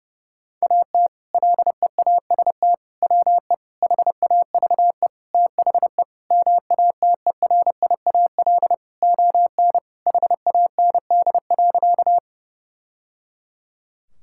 Morse code, heard as a keyed tone, about 30 wpm, 700 Hz. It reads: AT LEAST WE HAVE THE MATERIAL ON HAND.
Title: Sentences from Top 1000 Words - Review Part 5 - 30wpm